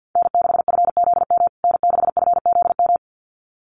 Morse code, heard as a keyed tone, about 50 wpm, 700 Hz. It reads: N6FZM N6FZM
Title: USA Call Signs Repeated 2x 50wpm